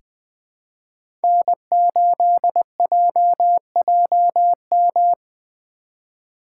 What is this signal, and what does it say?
Morse code, heard as a keyed tone, about 20 wpm, 700 Hz. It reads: N8JJM
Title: 2023 POTA Hunter Call Signs - 20wpm